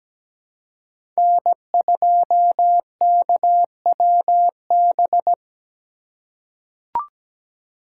Morse code, heard as a keyed tone, about 17 wpm, 700 Hz. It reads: N2KWB
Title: USA Call Signs - 5 Characters - 4x Character Spacing - 17wpm